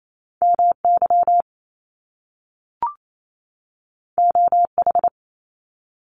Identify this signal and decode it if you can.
Morse code, heard as a keyed tone, about 28 wpm, 700 Hz. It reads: M Y O H
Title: Top Two Letter Combinations in Words 28wpm